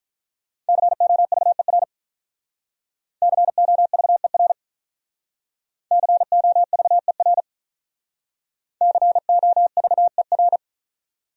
Morse code, heard as a keyed tone, about 45 wpm, 700 Hz. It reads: COVER COVER COVER COVER